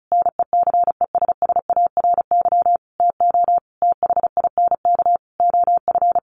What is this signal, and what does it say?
Morse code, heard as a keyed tone, about 34 wpm, 700 Hz. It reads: NECESSARY TO THINK OF
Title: Sentences from Top 1300 Words 35wpm